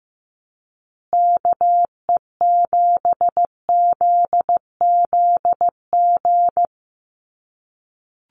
Morse code, heard as a keyed tone, about 15 wpm, 700 Hz. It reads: KE7ZZG
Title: USA Call Signs Repeated 2x 15wpm